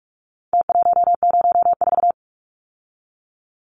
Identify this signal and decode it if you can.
Morse code, heard as a keyed tone, about 44 wpm, 700 Hz. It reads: T104
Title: Numbers - Contest Style - 4-Digit 45wpm